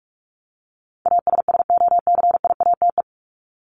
Morse code, heard as a keyed tone, about 46 wpm, 700 Hz. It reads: ASSOCIATE
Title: ED Suffix - Top 100 Words with Base Words 45wpm